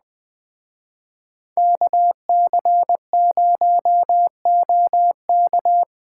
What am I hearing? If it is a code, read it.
Morse code, heard as a keyed tone, about 20 wpm, 700 Hz. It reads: KC0OK
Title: Speed Racing - USA 2x2 Call Signs 17wpm